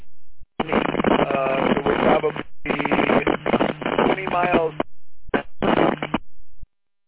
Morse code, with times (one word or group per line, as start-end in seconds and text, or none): none